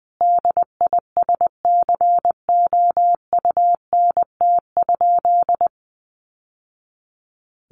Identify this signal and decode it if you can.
Morse code, heard as a keyed tone, about 20 wpm, 700 Hz. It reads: DISCOUNT?
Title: Sentences from Top 1200 Words 20wpm